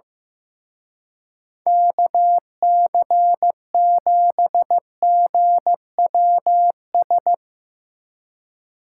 Morse code, heard as a keyed tone, about 16 wpm, 700 Hz. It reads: KC7GWS
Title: Speed Racing - USA 2x3 Call Signs 15wpm